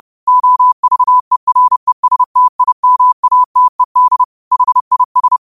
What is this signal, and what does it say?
Morse code, heard as a keyed tone, about 30 wpm, 1000 Hz. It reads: OVERESTIMATED HIS